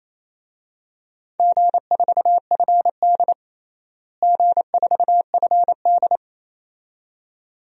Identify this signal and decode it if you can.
Morse code, heard as a keyed tone, about 27 wpm, 700 Hz. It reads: G4FD G4FD
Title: DXCC-100 Call Signs - 4 Characters - Repeated 2x - 28wpm